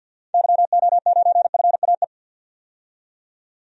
Morse code, heard as a keyed tone, about 52 wpm, 700 Hz. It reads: YO9FAE
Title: DXCC-100 Call Signs - 6 Characters - 50wpm